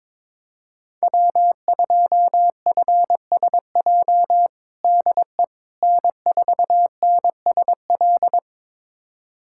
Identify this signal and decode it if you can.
Morse code, heard as a keyed tone, about 22 wpm, 700 Hz. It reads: W2FSJ DE N4NSL
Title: USA Callsign DE Callsign - 5 Characters - 22wpm